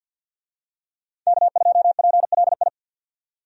Morse code, heard as a keyed tone, about 51 wpm, 700 Hz. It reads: K2PLI
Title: USA Call Signs - 5 Characters - 50wpm